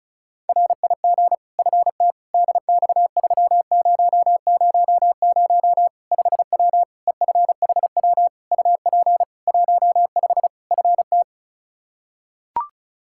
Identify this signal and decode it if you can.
Morse code, heard as a keyed tone, about 36 wpm, 700 Hz. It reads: RIG FT DX3000 5W EFHW UP 15 FT
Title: K1USN SST QSO Practice 35wpm